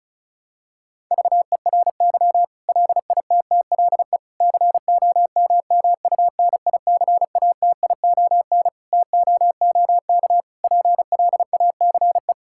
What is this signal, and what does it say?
Morse code, heard as a keyed tone, about 36 wpm, 700 Hz. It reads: VERY LITTLE COMMUNICATION TOOK PLACE
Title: Sentences from Top 1000 Words - Review Part 6 - 35wpm